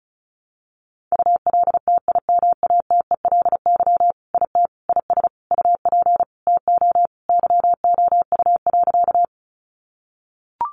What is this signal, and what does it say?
Morse code, heard as a keyed tone, about 34 wpm, 700 Hz. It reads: ULTIMATELY IT IS UP TO YOU.
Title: Sentences from Top 1200 Words 35wpm